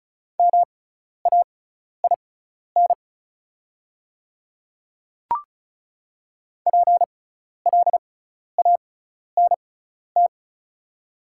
Morse code, heard as a keyed tone, about 36 wpm, spaced as for 12 wpm, 690 Hz. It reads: MAIN PLANT
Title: 6x Character Spacing - Top 300 Words - 35wpm